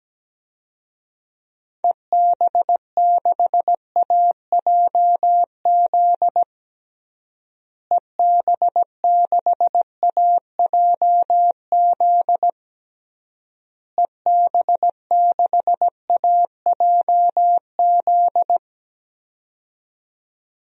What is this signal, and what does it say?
Morse code, heard as a keyed tone, about 17 wpm, 700 Hz. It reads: EB6AJZ EB6AJZ EB6AJZ
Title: DXCC-100 Call Signs - 6 Characters - Repeated 3x - 17wpm